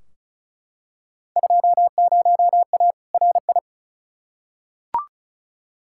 Morse code, heard as a keyed tone, about 36 wpm, 700 Hz. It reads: 20A RI
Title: Field Day Exchanges Encore - 35wpm